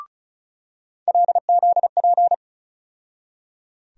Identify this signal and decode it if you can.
Morse code, heard as a keyed tone, about 36 wpm, 700 Hz. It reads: LZP
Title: USA Call Sign Postfixes - 3 Letters 35wpm